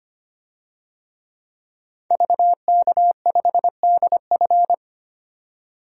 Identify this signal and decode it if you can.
Morse code, heard as a keyed tone, about 25 wpm, 700 Hz. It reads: VK5DF
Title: DXCC-100 Call Signs - 5 Characters - 25wpm